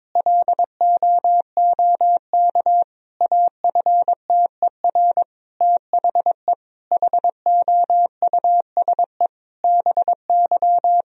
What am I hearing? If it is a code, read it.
Morse code, heard as a keyed tone, about 22 wpm, 700 Hz. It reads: LOOK AFTER THE HOUSE BY
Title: Sentences from Top 1000 Words - Review Part 4 - 22wpm